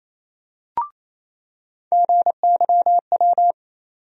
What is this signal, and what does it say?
Morse code, heard as a keyed tone, about 28 wpm, 700 Hz. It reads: GYW
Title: USA Call Sign Postfixes - 3 Letters 28wpm